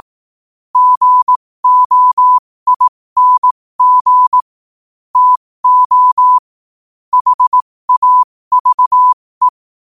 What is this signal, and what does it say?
Morse code, heard as a keyed tone, about 18 wpm, spaced as for 15 wpm, 1000 Hz. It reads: GOING TO HAVE
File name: qotdfiles / 15wpm